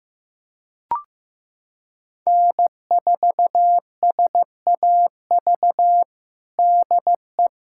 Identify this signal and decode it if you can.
Morse code, heard as a keyed tone, about 15 wpm, 700 Hz. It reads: N4SAV DE
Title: USA Callsign DE Callsign - 5 Characters - 15wpm